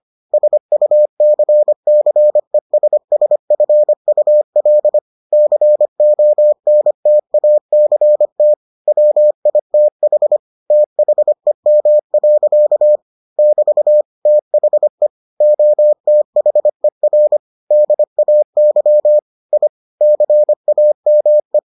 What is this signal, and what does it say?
Morse code, heard as a keyed tone, about 25 wpm, 600 Hz. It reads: SUCCESSFUL CONTACT WITH THEM. <BT> THE OTHER DAY I CAME